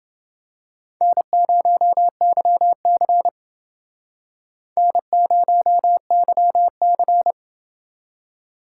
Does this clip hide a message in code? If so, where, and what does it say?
Yes: Morse code from 1.0 to 7.3 s, heard as a keyed tone, about 29 wpm, 700 Hz. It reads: N0YC N0YC